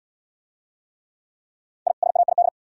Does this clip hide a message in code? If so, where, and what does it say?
Yes: Morse code from 1.9 to 2.5 s, heard as a keyed tone, about 75 wpm, 700 Hz. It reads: I SAID